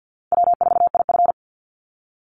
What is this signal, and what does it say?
Morse code, heard as a keyed tone, about 50 wpm, 700 Hz. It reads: W4IF
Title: Easy Call Signs Repeated 3x - 50wpm